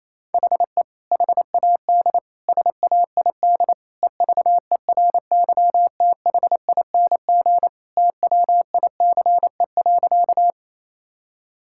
Morse code, heard as a keyed tone, about 28 wpm, 700 Hz. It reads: HE HAD SAID EVERYTHING TWICE.
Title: Sentences from Top 1600 Words 28wpm